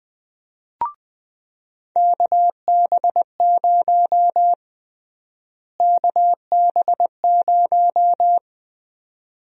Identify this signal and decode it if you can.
Morse code, heard as a keyed tone, about 20 wpm, 700 Hz. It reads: KB0 KB0
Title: USA Call Sign Prefixes - Repeated 2x - 20wpm